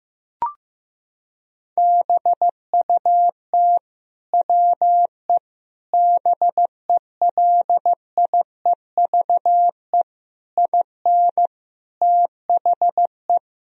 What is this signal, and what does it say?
Morse code, heard as a keyed tone, about 15 wpm, 700 Hz. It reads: BUT WE BELIEVE IN THE
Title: Sentences from Top 1000 Words - Review Part 3 - 15wpm